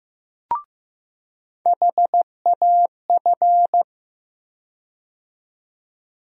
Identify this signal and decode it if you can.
Morse code, heard as a keyed tone, about 15 wpm, 700 Hz. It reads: HAF
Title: USA Call Sign Postfixes - 3 Letters 15wpm